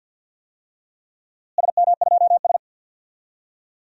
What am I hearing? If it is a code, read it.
Morse code, heard as a keyed tone, about 50 wpm, 700 Hz. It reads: SM2S
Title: DXCC-100 Call Signs - 4 Characters - Repeated 3x - 50wpm